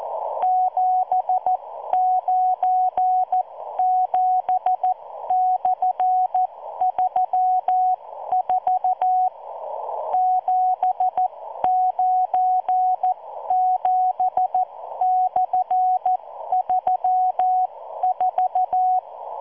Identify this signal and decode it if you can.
Morse code, heard as a keyed tone, about 14 wpm, spaced as for 12 wpm, 740 Hz. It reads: 797/34 797/34